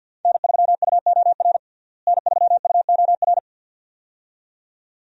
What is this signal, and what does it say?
Morse code, heard as a keyed tone, about 50 wpm, 700 Hz. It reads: N3UOR N3UOR
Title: USA Call Signs - 5 Characters - Repeated 2x - 50wpm